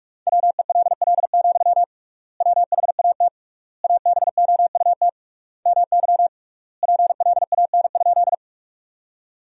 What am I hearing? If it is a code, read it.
Morse code, heard as a keyed tone, about 46 wpm, 700 Hz. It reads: WELL, WHAT ABOUT MY PLAN?
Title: 3x Word-Spacing - Sentences from Top 500 Words - 45wpm